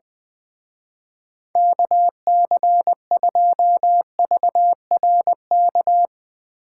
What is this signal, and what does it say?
Morse code, heard as a keyed tone, about 20 wpm, 700 Hz. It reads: KC2VRK
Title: Speed Racing - USA 2x3 Call Signs 20wpm